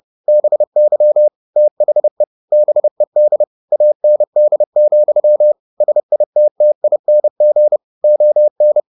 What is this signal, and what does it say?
Morse code, heard as a keyed tone, about 30 wpm, 600 Hz. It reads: BY THE BED AND, SITTING ON
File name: the old man and the sea / the old man and the sea 1 30wpm